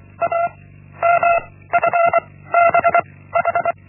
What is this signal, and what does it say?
Morse code, heard as a keyed tone, about 24 wpm, 680 Hz. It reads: A M F B H
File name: Linksmakalnis-1